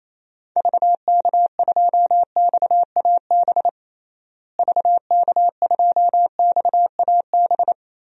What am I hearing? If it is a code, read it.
Morse code, heard as a keyed tone, about 27 wpm, 700 Hz. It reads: VK2XAB VK2XAB
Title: DXCC-100 Call Signs - 6 Characters - Repeated 2x - 28wpm